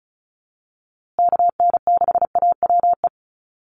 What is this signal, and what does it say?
Morse code, heard as a keyed tone, about 34 wpm, 700 Hz. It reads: KN6AWE
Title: USA Call Signs Repeated 3x 35wpm